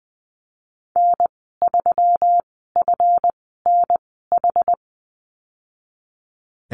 Morse code, heard as a keyed tone, about 19 wpm, spaced as for 14 wpm, 700 Hz. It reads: N3FNH
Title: 140 Lesson 23g - Letter f - Call Signs 20wpm